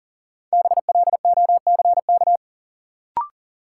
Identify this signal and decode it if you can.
Morse code, heard as a keyed tone, about 40 wpm, 700 Hz. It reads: BLOCK
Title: Sentences from Top 1600 Words - New Word Drill Alt - 40wpm